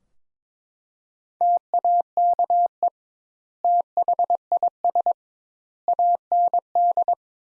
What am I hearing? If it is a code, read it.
Morse code, heard as a keyed tone, about 22 wpm, 700 Hz. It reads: TAKE THIS AND
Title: Sentences Top 200 Words - 22wpm